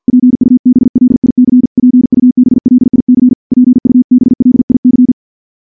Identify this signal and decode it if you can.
Morse code, heard as a keyed tone, about 48 wpm, 250 Hz. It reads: PUBLIC PUBLIC PUBLIC
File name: morse / morse-training-50WPM